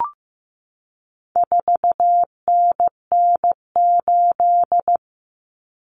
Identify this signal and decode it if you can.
Morse code, heard as a keyed tone, about 15 wpm, 700 Hz. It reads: 4NN8
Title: Common Cut Numbers - 4-Digit 15wpm